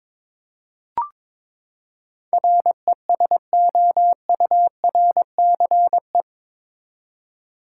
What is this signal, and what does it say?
Morse code, heard as a keyed tone, about 22 wpm, 700 Hz. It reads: RESOURCE
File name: practice / 22wpm / Eight Letter Words - 22wpm